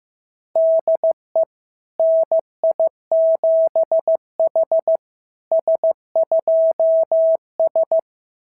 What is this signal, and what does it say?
Morse code, heard as a keyed tone, about 15 wpm, 690 Hz. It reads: DE NI7H S2S
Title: SOTA QSO Practice 15wpm